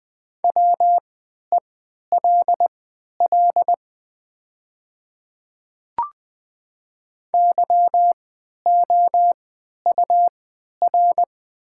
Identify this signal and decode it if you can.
Morse code, heard as a keyed tone, about 20 wpm, spaced as for 11 wpm, 700 Hz. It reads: WELL YOUR
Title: 3x Character Spacing - Top 200 Words - 20wpm